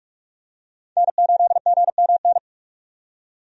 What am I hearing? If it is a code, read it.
Morse code, heard as a keyed tone, about 41 wpm, 700 Hz. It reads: N8GMN
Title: USA Call Signs - 5 Characters - 45wpm